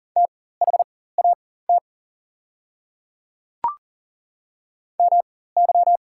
Morse code, heard as a keyed tone, about 38 wpm, spaced as for 18 wpm, 690 Hz. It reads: THAT MY
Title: 4x Character Spacing - Top 100 Words - 40wpm